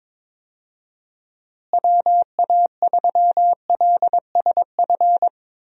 Morse code, heard as a keyed tone, about 22 wpm, 700 Hz. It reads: WA3LSF
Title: USA Call Signs - 6 Characters - Repeated 2x - 22wpm